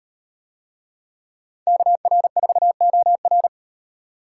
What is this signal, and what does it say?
Morse code, heard as a keyed tone, about 35 wpm, 700 Hz. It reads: KR4OR